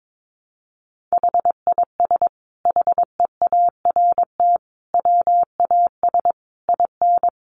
Morse code, heard as a keyed tone, about 22 wpm, 700 Hz. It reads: HIS HEART WAS IN
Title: Sentences from Top 1400 Words 22wpm